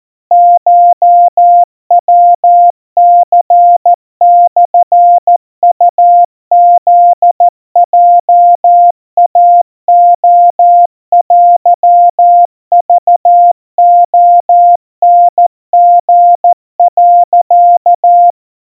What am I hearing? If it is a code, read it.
Morse code (Japanese Wabun code), heard as a keyed tone, about 14 wpm, 700 Hz. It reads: コヤニモウフヲイレテクレタリ、
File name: kamyu 50 cpm 700 Hz Crispy